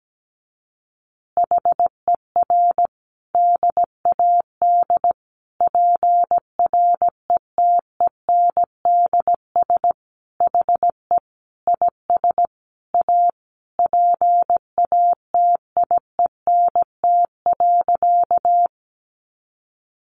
Morse code, heard as a keyed tone, about 17 wpm, 700 Hz. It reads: HER DAD PRETENDS HE IS A PATIENT.
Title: Additional Easy Phrases 17wpm